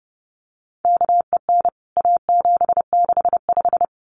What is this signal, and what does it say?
Morse code, heard as a keyed tone, about 30 wpm, 700 Hz. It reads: KEN A765
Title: CWT Exchange 30wpm